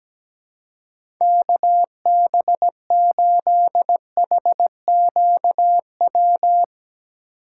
Morse code, heard as a keyed tone, about 17 wpm, 700 Hz. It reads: KB8HQW
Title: USA Call Signs - 6 Characters - 2x Character Spacing - 17wpm